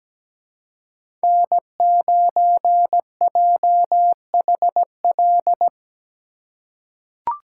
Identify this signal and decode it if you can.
Morse code, heard as a keyed tone, about 17 wpm, 700 Hz. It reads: N9JHL
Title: K1USN SST QSO Practice - 17wpm